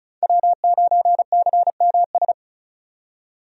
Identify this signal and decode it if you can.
Morse code, heard as a keyed tone, about 34 wpm, 700 Hz. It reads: W9CMS
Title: USA Call Signs - 5 Characters - 35wpm